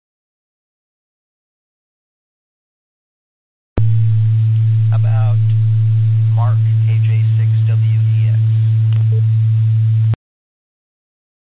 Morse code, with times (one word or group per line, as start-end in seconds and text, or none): none